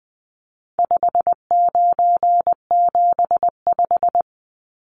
Morse code, heard as a keyed tone, about 20 wpm, 700 Hz. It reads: 5975
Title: Numbers - 4-Digit 20wpm